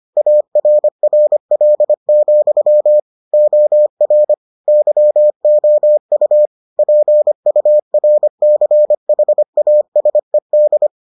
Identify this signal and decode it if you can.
Morse code, heard as a keyed tone, about 25 wpm, 600 Hz. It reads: ARRL, OR YOU PURCHASED